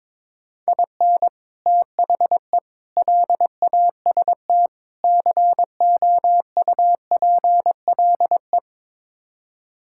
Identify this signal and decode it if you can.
Morse code, heard as a keyed tone, about 22 wpm, 700 Hz. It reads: IN THE LAST COUPLE
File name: practice / 22wpm / Build A Phrase - 5-Word Phrases - 22wpm